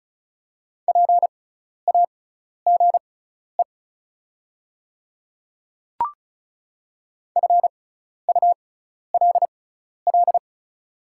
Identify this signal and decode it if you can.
Morse code, heard as a keyed tone, about 36 wpm, spaced as for 12 wpm, 690 Hz. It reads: PAGE FULL
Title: 6x Character Spacing - Top 500 Words - 35wpm